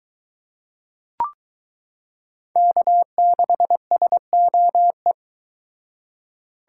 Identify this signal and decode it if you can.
Morse code, heard as a keyed tone, about 22 wpm, 700 Hz. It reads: K6SOE